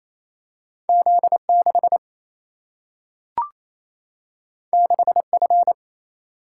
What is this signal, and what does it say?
Morse code, heard as a keyed tone, about 28 wpm, 700 Hz. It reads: Z 6 6 F
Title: Single-Letter Single-Number 28wpm